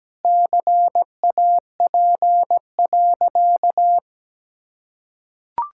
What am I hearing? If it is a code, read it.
Morse code, heard as a keyed tone, about 17 wpm, 700 Hz. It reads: CAP.
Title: Sentences from Top 1600 Words 17wpm